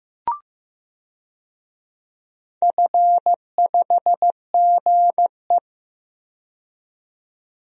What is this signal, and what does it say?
Morse code, heard as a keyed tone, about 15 wpm, 700 Hz. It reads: F5GE
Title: DXCC-100 Call Signs - 4 Characters - 15wpm